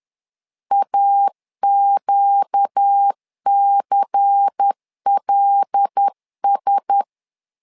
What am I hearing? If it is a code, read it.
Morse code, heard as a keyed tone, about 10 wpm, 800 Hz. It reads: AQCLS
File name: LEZIONI / 5 LBVC / 5 LBVC 10wpm 2349